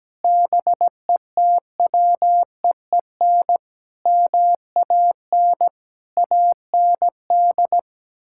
Morse code, heard as a keyed tone, about 17 wpm, 700 Hz. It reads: BETWEEN MAN AND
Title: Sentences from Top 1600 Words 17wpm